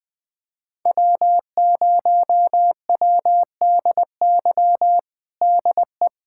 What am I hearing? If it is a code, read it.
Morse code, heard as a keyed tone, about 20 wpm, 700 Hz. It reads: W0WDY DE
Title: USA Callsign DE Callsign - 5 Characters - 20wpm